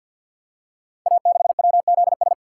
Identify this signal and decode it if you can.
Morse code, heard as a keyed tone, about 48 wpm, 700 Hz. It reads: A6WZS
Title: Code Groups - 5 Letter:Numbers - 50wpm